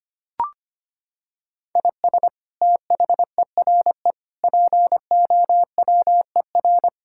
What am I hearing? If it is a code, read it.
Morse code, heard as a keyed tone, about 25 wpm, 700 Hz. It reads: IS THERE POWER
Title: Sentences from Top 1000 Words - Review Part 6 - 25wpm